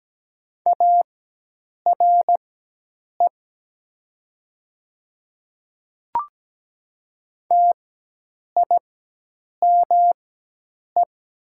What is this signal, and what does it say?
Morse code, heard as a keyed tone, about 17 wpm, spaced as for 8 wpm, 700 Hz. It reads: ARE TIME
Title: 4x Character Spacing - Top 100 Words - 17wpm